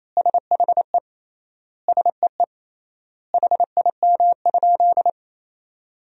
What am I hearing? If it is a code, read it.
Morse code, heard as a keyed tone, about 28 wpm, 700 Hz. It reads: SHE SEE HIM?
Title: 3x Word-Spacing - Sentences from Top 100 Words - 28wpm